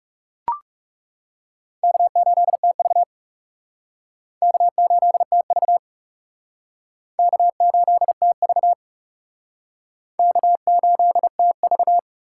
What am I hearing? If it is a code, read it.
Morse code, heard as a keyed tone, about 37 wpm, 700 Hz. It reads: K8TV K8TV K8TV K8TV
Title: Speed Racing - USA 1x2 Call Signs 30wpm